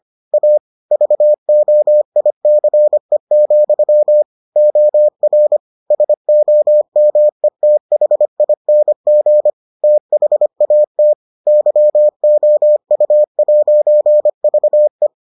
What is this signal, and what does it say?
Morse code, heard as a keyed tone, about 25 wpm, 600 Hz. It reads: A VOICE, OR SOMETHING THAT YOU'VE